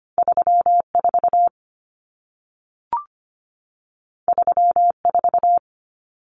Morse code, heard as a keyed tone, about 25 wpm, 700 Hz. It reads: 34 34
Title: Common Cut Numbers - 2-Digit 25wpm